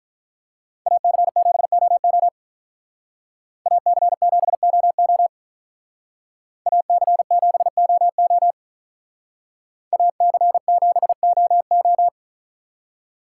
Morse code, heard as a keyed tone, about 43 wpm, 700 Hz. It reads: AC7OO AC7OO AC7OO AC7OO
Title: Speed Racing - USA 2x2 Call Signs 35wpm